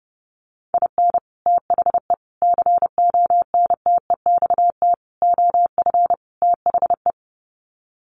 Morse code, heard as a keyed tone, about 28 wpm, 700 Hz. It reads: IN THE CONTEXT OF THE
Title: Sets of 5 Words 30wpm